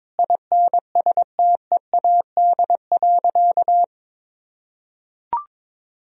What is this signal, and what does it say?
Morse code, heard as a keyed tone, about 22 wpm, 700 Hz. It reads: INSTEAD.
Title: Sentences from Top 1000 Words - Review Part 4 - 22wpm